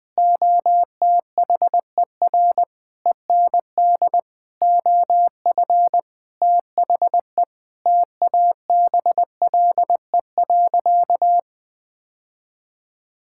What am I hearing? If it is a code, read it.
Morse code, heard as a keyed tone, about 20 wpm, 700 Hz. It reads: OTHER END OF THE TABLE.
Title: Sentences with 6-Word Phrases - 20wpm